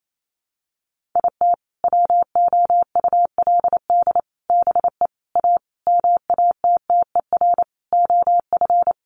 Morse code, heard as a keyed tone, about 28 wpm, 700 Hz. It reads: IT WOULD BE A MATTER OF
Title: Sentences from Top 1400 Words 28wpm